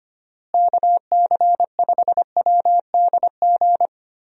Morse code, heard as a keyed tone, about 25 wpm, 700 Hz. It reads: KC5WDG
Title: USA Call Signs - 6 Characters - Repeated 3x - 25wpm